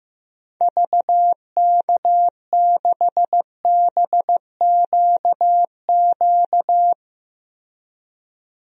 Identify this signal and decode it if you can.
Morse code, heard as a keyed tone, about 15 wpm, 700 Hz. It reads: VK6BQQ
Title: DXCC-100 Call Signs - 6 Characters - Repeated 2x - 15wpm